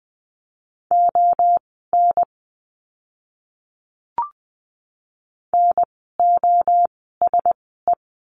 Morse code, heard as a keyed tone, about 20 wpm, spaced as for 14 wpm, 700 Hz. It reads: ON NOSE
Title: 036 Lesson 7f - Letter s - Words 20wpm